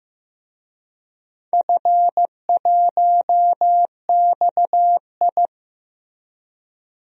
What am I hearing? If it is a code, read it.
Morse code, heard as a keyed tone, about 15 wpm, 700 Hz. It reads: F1XI
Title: DXCC-100 Call Signs - 4 Characters - Repeated 3x - 15wpm